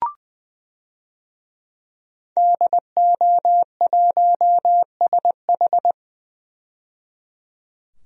Morse code, heard as a keyed tone, about 20 wpm, 700 Hz. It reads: DO1SH
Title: DXCC-100 Call Signs - 5 Characters - 20wpm